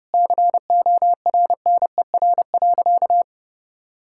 Morse code, heard as a keyed tone, about 30 wpm, 700 Hz. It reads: CORNER.
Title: Sentences from Top 1600 Words 30wpm